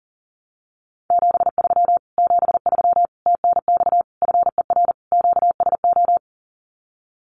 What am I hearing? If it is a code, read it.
Morse code, heard as a keyed tone, about 40 wpm, 700 Hz. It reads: 73 73 TNX FER QSO
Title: Common QSO Phrases Encore 40wpm